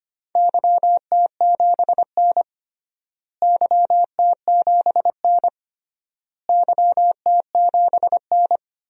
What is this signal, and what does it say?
Morse code, heard as a keyed tone, about 25 wpm, 700 Hz. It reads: YT7N YT7N YT7N
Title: DXCC-100 Call Signs - 4 Characters - Repeated 3x - 25wpm